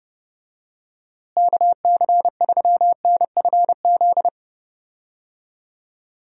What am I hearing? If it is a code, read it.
Morse code, heard as a keyed tone, about 30 wpm, 700 Hz. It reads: KC3NFZ
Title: USA Call Signs - 6 Characters - Repeated 3x - 30wpm